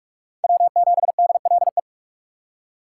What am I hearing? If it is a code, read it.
Morse code, heard as a keyed tone, about 57 wpm, 700 Hz. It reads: W7DLE